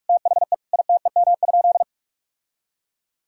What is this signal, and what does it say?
Morse code, heard as a keyed tone, about 41 wpm, 700 Hz. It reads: THE ITEM?
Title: Sentences from Top 1000 Words - Review Part 6 - 45wpm